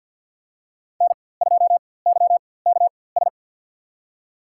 Morse code, heard as a keyed tone, about 51 wpm, spaced as for 23 wpm, 700 Hz. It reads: N2YKS
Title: USA Call Signs - 5 Characters - 4x Character Spacing - 50wpm